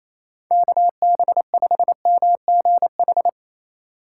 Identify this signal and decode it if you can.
Morse code, heard as a keyed tone, about 27 wpm, 700 Hz. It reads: KB5MGH
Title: USA Call Signs - 6 Characters - 2x Character Spacing - 28wpm